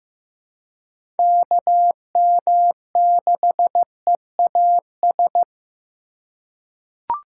Morse code, heard as a keyed tone, about 15 wpm, 700 Hz. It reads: KM6EAS
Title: USA Call Signs - 6 Characters - 4x Character Spacing - 15wpm